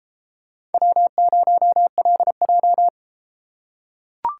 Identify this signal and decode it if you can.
Morse code, heard as a keyed tone, about 32 wpm, 700 Hz. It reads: W0LJ